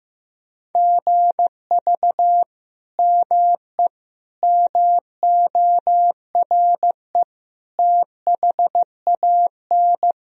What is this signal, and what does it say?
Morse code, heard as a keyed tone, about 15 wpm, 700 Hz. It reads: GV ME MORE THAN
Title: Sentences from Top 200 Words and Abbreviations 15wpm